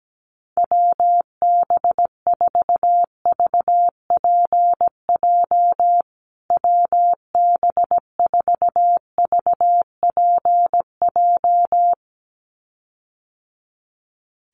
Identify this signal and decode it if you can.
Morse code, heard as a keyed tone, about 17 wpm, 700 Hz. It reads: WB4VPJ WB4VPJ
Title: USA Call Signs Repeated 2x 17wpm